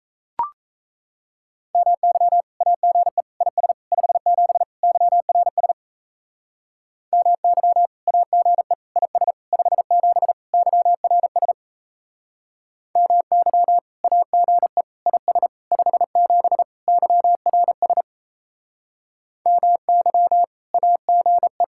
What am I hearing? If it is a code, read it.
Morse code, heard as a keyed tone, about 34 wpm, 700 Hz. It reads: MY AGE IS 57 YRS MY AGE IS 57 YRS MY AGE IS 57 YRS MY AGE
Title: Speed-Racing Common QSO Phrases - 28wpm